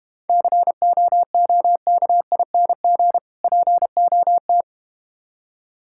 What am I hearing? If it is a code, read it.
Morse code, heard as a keyed tone, about 31 wpm, 700 Hz. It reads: COOKING POT
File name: practice / 25wpm / Difficult Words - Speed-Racing with Hints - 25wpm